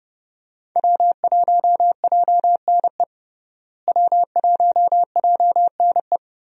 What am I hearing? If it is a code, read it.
Morse code, heard as a keyed tone, about 32 wpm, 700 Hz. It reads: W1JNE W1JNE
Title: USA Call Signs - 5 Characters - Repeated 3x - 30wpm